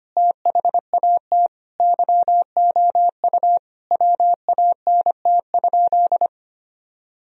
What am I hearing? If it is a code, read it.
Morse code, heard as a keyed tone, about 25 wpm, 700 Hz. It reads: THAT YOU WANT?
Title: Sentences from Top 1000 Words - Review Part 4 - 25wpm